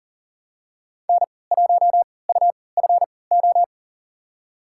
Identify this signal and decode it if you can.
Morse code, heard as a keyed tone, about 40 wpm, spaced as for 23 wpm, 700 Hz. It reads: N1UFO
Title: USA Call Signs - 5 Characters - 3x Character Spacing - 40wpm